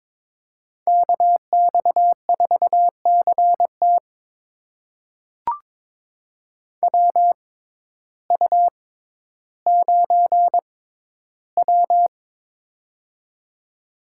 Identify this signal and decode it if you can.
Morse code, heard as a keyed tone, about 22 wpm, spaced as for 12 wpm, 700 Hz. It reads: KX4CT WU9W
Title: USA Call Signs - 6x Character Spacing - 22wpm